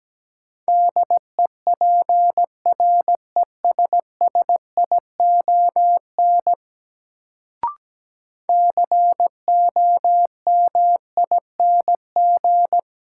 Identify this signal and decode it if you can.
Morse code, heard as a keyed tone, about 17 wpm, 700 Hz. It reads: DEPRESSION COMING
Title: Sentences from Top 1300 Words - New Word Drill Alt - 17wpm